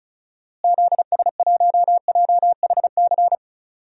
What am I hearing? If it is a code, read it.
Morse code, heard as a keyed tone, about 36 wpm, 700 Hz. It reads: ZS1JHC
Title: DXCC-100 Call Signs - 6 Characters - Repeated 3x - 35wpm